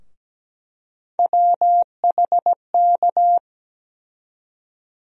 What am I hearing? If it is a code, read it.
Morse code, heard as a keyed tone, about 17 wpm, 700 Hz. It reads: WHK
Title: USA Call Sign Postfixes - 3 Letters 17wpm